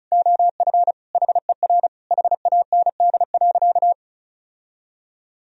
Morse code, heard as a keyed tone, about 34 wpm, 700 Hz. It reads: OF HER HAND.
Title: Sentences from Top 1000 Words - Review Part 2 - 35wpm